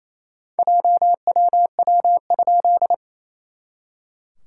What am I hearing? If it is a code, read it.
Morse code, heard as a keyed tone, about 28 wpm, 690 Hz. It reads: JWW?
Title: SOTA QSO Practice 28wpm